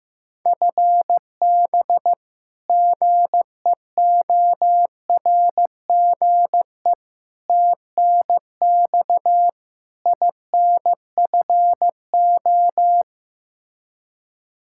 Morse code, heard as a keyed tone, about 15 wpm, 700 Hz. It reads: FB GEORGE TNX INFO
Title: K1USN SST QSO Practice 15wpm